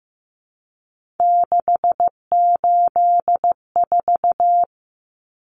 Morse code, heard as a keyed tone, about 15 wpm, 700 Hz. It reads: 684
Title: Numbers - 3-Digit 15wpm